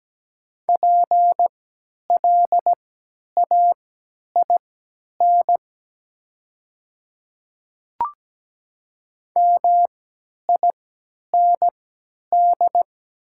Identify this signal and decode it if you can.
Morse code, heard as a keyed tone, about 17 wpm, spaced as for 10 wpm, 700 Hz. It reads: PLAIN MIND
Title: 3x Character Spacing - Top 500 Words - 17wpm